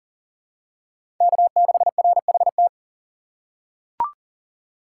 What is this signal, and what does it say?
Morse code, heard as a keyed tone, about 38 wpm, 700 Hz. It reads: K6RHT
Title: USA Call Signs - 5 Characters - 2x Character Spacing - 40wpm